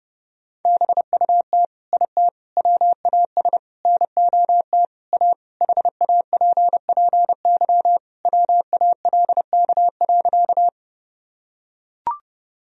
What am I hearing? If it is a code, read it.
Morse code, heard as a keyed tone, about 30 wpm, 700 Hz. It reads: BUT IT WAS NOT A HAPPY WALK.
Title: Sentences from Top 1000 Words - Review Part 1 - 30wpm